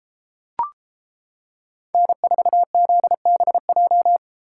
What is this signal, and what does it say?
Morse code, heard as a keyed tone, about 33 wpm, 700 Hz. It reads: N4ZBJ